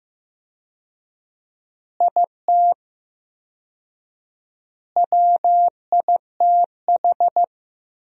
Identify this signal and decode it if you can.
Morse code, heard as a keyed tone, about 16 wpm, 700 Hz. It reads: IT WITH
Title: 4x Word-Spacing - Sentences from Top 500 Words - 15wpm